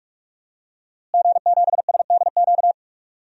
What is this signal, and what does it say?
Morse code, heard as a keyed tone, about 46 wpm, 700 Hz. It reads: G7SDQ
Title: DXCC-100 Call Signs - 5 Characters - Repeated 3x - 45wpm